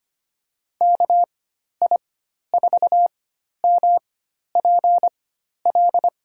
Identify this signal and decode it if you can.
Morse code, heard as a keyed tone, about 25 wpm, spaced as for 12 wpm, 700 Hz. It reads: KI4MPL
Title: USA Call Signs - 6 Characters - 4x Character Spacing - 25wpm